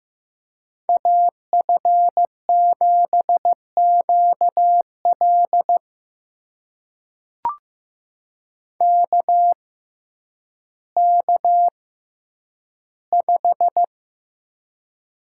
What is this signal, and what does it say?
Morse code, heard as a keyed tone, about 15 wpm, spaced as for 9 wpm, 700 Hz. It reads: AF7QL KK5
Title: USA Call Signs - 5 Characters - 6x Character Spacing - 15wpm